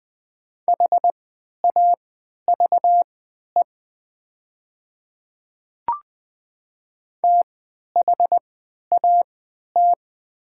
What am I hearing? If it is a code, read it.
Morse code, heard as a keyed tone, about 20 wpm, spaced as for 11 wpm, 700 Hz. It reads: HAVE THAT
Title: 3x Character Spacing - Top 100 Words - 20wpm